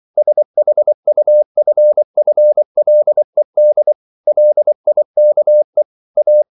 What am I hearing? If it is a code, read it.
Morse code, heard as a keyed tone, about 24 wpm, 600 Hz. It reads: SHUFFLED LIKE A